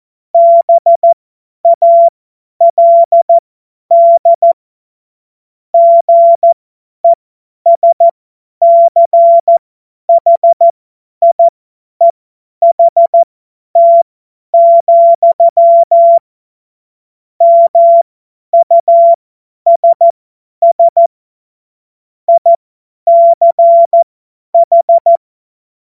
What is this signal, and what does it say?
Morse code, heard as a keyed tone, about 14 wpm, spaced as for 10 wpm, 680 Hz. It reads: BALD GESCHIEHT, MUSS ICH
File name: Morsekurs / Witze / Witz 27. Dezember 2022 50 F70